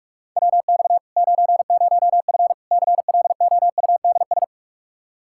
Morse code, heard as a keyed tone, about 46 wpm, 700 Hz. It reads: WX 90F CLOUDS
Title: Practice QSO - Exchange 2 - Info - 45wpm